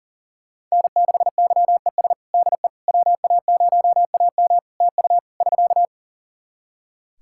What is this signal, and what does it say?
Morse code, heard as a keyed tone, about 37 wpm, 700 Hz. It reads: N6YES DE WA0AM TU <SK>
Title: Practice QSO - Exchange 4 - Ending - 40wpm